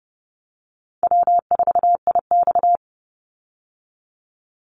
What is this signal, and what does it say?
Morse code, heard as a keyed tone, about 30 wpm, 700 Hz. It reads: W4IX
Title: Call Signs Repeated 2x - 30wpm